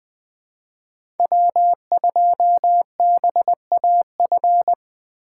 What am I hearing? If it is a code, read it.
Morse code, heard as a keyed tone, about 20 wpm, 700 Hz. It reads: W2BAF
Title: USA Call Signs - 5 Characters - 4x Character Spacing - 20wpm